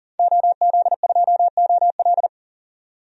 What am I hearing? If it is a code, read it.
Morse code, heard as a keyed tone, about 40 wpm, 700 Hz. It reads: OZ2OL
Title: DXCC-100 Call Signs - 5 Characters - 40wpm